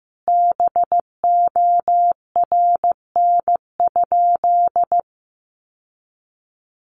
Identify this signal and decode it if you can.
Morse code, heard as a keyed tone, about 15 wpm, 700 Hz. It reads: BORN?
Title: Sentences from Top 1000 Words 15wpm